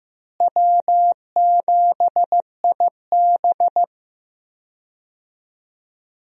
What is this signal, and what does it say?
Morse code, heard as a keyed tone, about 15 wpm, 700 Hz. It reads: W7IB
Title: USA Call Signs - 4 Characters - Repeated 2x - 15wpm